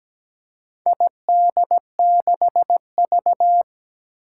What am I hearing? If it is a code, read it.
Morse code, heard as a keyed tone, about 17 wpm, 700 Hz. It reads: ID6V
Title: Four Challenging Letter-Number Combinations 17wpm